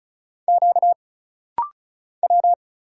Morse code, heard as a keyed tone, about 36 wpm, 700 Hz. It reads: Q W
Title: Single Letter-Number Alt - Rapid-Fire 35wpm